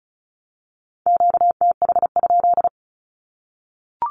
Morse code, heard as a keyed tone, about 34 wpm, 700 Hz. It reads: QTH?
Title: Q-Codes 35wpm